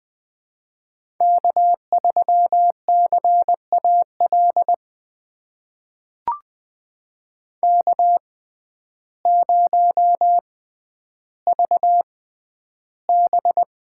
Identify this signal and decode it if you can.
Morse code, heard as a keyed tone, about 20 wpm, spaced as for 11 wpm, 700 Hz. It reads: K3CAL K0VB
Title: USA Call Signs - 5 Characters - 6x Character Spacing - 20wpm